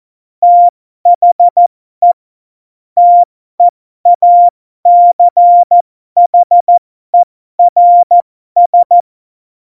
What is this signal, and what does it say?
Morse code, heard as a keyed tone, about 14 wpm, spaced as for 12 wpm, 700 Hz. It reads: THE TEACHERS
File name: Morsekurs / Witze